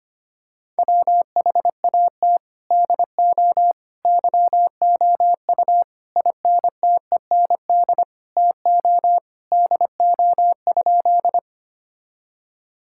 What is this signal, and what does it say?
Morse code, heard as a keyed tone, about 25 wpm, 700 Hz. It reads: WHAT DO YOU INTEND TO DO?
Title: Sentences from Top 1600 Words 25wpm